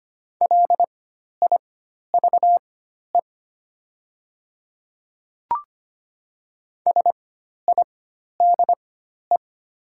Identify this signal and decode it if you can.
Morse code, heard as a keyed tone, about 25 wpm, spaced as for 12 wpm, 700 Hz. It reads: LIVE SIDE
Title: 4x Character Spacing - Top 300 Words - 25wpm